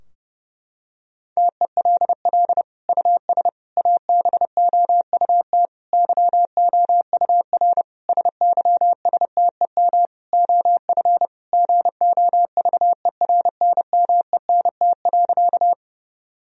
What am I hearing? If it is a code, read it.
Morse code, heard as a keyed tone, about 30 wpm, 700 Hz. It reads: TELL US ABOUT YOUR SYSTEM OF GOVERNMENT.
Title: Sentences from Top 1000 Words - Review Part 3 - 30wpm